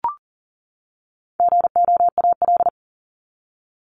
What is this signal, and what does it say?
Morse code, heard as a keyed tone, about 40 wpm, 700 Hz. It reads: GOAL
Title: Top 751-1000 Words 40wpm